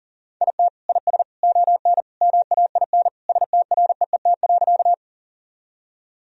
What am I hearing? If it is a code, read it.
Morse code, heard as a keyed tone, about 37 wpm, 700 Hz. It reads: IT IS ON MAIN STREET.
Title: Sentences from Top 1000 Words - Review Part 2 - 40wpm